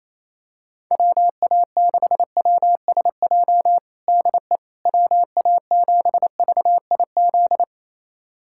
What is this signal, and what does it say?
Morse code, heard as a keyed tone, about 28 wpm, 700 Hz. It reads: WA6WSJ DE WA7VIZ
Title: USA Callsign DE Callsign - 6 Characters - 28wpm